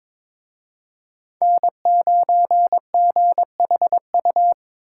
Morse code, heard as a keyed tone, about 26 wpm, 700 Hz. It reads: N9GHU
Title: Speed Racing - USA Call Signs 22wpm